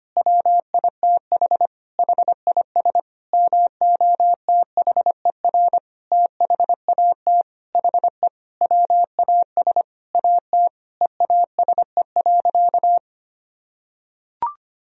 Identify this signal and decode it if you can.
Morse code, heard as a keyed tone, about 25 wpm, 700 Hz. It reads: WITH HIS MOTHER THAT HE WAS AT EASE.
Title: Sentences from Top 1000 Words - Review Part 1 - 25wpm